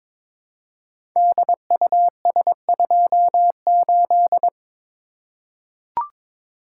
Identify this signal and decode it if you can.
Morse code, heard as a keyed tone, about 22 wpm, 700 Hz. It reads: DUS28
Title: Code Groups - 5 Letter:Numbers - 22wpm